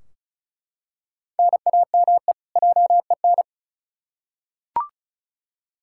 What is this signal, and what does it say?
Morse code, heard as a keyed tone, about 36 wpm, 700 Hz. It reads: NAME JEN
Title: K1USN SST QSO Practice 35wpm